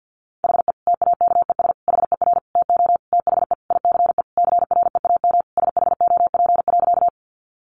Morse code, heard as a keyed tone, about 50 wpm, 700 Hz. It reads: HE TAKES HER TO THE ICE CREAM SHOP.
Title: Additional Easy Phrases 50wpm